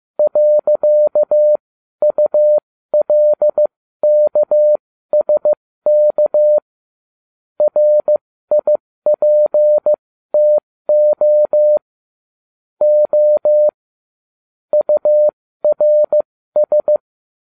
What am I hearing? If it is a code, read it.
Morse code, heard as a keyed tone, about 15 wpm, spaced as for 13 wpm, 600 Hz. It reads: .ULKSK RIPTO O URS
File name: cw / ubacw12 1